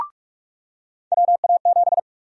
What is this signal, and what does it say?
Morse code, heard as a keyed tone, about 44 wpm, 700 Hz. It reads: WA7
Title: USA Call Sign Prefixes - 45wpm